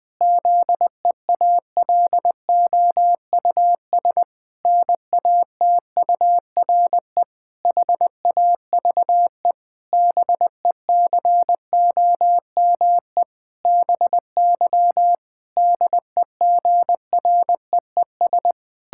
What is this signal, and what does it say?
Morse code, heard as a keyed tone, about 20 wpm, 700 Hz. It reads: ZEALOUS NATURE HAVE BECOME BY DEGREES
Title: Pangram Challenge v2 - 20wpm